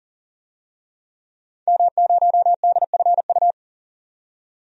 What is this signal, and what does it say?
Morse code, heard as a keyed tone, about 40 wpm, 700 Hz. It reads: M0DFU
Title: DXCC-100 Call Signs - 5 Characters - 40wpm